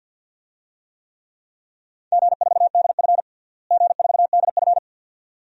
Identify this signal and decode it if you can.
Morse code, heard as a keyed tone, about 52 wpm, 700 Hz. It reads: G4DF G4DF
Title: DXCC-100 Call Signs - 4 Characters - Repeated 2x - 50wpm